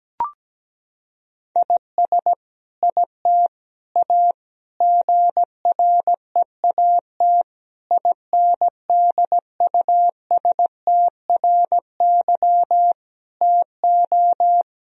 Morse code, heard as a keyed tone, about 17 wpm, 700 Hz. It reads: IS IT A GREAT INDUSTRY TO
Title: Sentences from Top 1000 Words - Review Part 4 - 17wpm